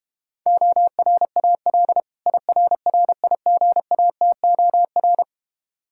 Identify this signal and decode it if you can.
Morse code, heard as a keyed tone, about 32 wpm, 700 Hz. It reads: ORAL IRRIGATOR